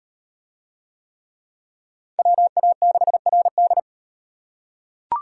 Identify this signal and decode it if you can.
Morse code, heard as a keyed tone, about 38 wpm, 700 Hz. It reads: WA6RD